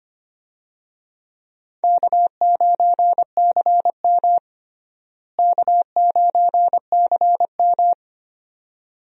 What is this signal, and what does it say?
Morse code, heard as a keyed tone, about 25 wpm, 700 Hz. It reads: K9CM K9CM
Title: USA Call Signs - 4 Characters - Repeated 2x - 25wpm